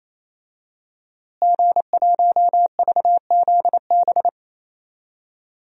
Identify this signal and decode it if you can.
Morse code, heard as a keyed tone, about 27 wpm, 700 Hz. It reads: G1VZB
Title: DXCC-100 Call Signs - 5 Characters - Repeated 3x - 28wpm